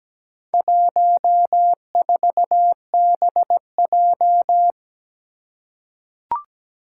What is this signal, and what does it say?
Morse code, heard as a keyed tone, about 17 wpm, 700 Hz. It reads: 14BJ
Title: Four Challenging Letter-Number Combinations 17wpm